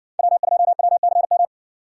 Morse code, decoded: W2WCM

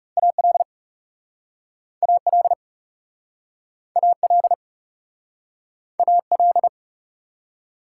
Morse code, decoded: AL AL AL AL

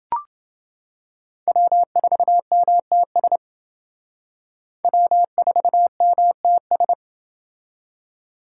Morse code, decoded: W4MTS W4MTS

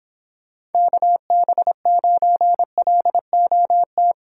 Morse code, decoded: KB9LOT

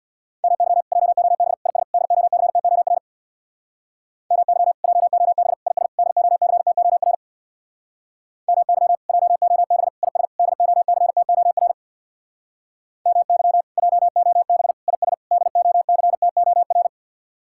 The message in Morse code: MY JOB IS DOCTOR MY JOB IS DOCTOR MY JOB IS DOCTOR MY JOB IS DOCTOR

